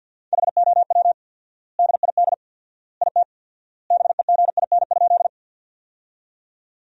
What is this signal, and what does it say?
Morse code, heard as a keyed tone, about 50 wpm, 700 Hz. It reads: HOW DID IT BEGIN?
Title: 4x Word-Spacing - Sentences from Top 300 Words - 50wpm